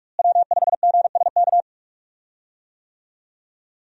Morse code, heard as a keyed tone, about 43 wpm, 700 Hz. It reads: W5GSK
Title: USA Call Signs - 5 Characters - Repeated 2x - 45wpm